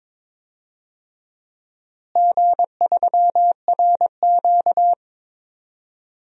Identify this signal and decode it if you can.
Morse code, heard as a keyed tone, about 22 wpm, 700 Hz. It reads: G3RQ
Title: DXCC-100 Call Signs - 4 Characters - Repeated 3x - 22wpm